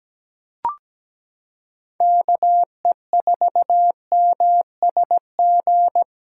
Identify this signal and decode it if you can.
Morse code, heard as a keyed tone, about 17 wpm, 700 Hz. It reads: KE4MSG
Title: USA Callsign DE Callsign - 6 Characters - 17wpm